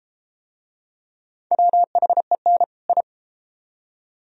Morse code, heard as a keyed tone, about 29 wpm, 700 Hz. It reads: WHEN I